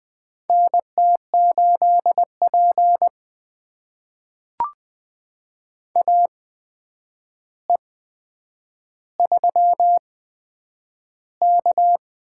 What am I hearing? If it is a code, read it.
Morse code, heard as a keyed tone, about 20 wpm, spaced as for 9 wpm, 700 Hz. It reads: NT8P AE3K
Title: USA Call Signs - 4 Characters - 8x Character Spacing - 20wpm